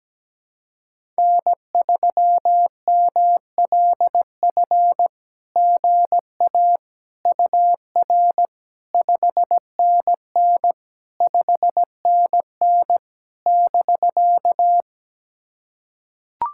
Morse code, heard as a keyed tone, about 17 wpm, 660 Hz. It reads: N3MLF GA UR 5NN 5NN <BK>
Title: SOTA QSO Practice 17wpm